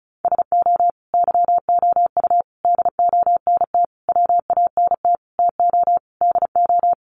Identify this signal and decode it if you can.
Morse code, heard as a keyed tone, about 34 wpm, 700 Hz. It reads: SO YOU DONT WANT TO DO
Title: Sentences from Top 1000 Words 35wpm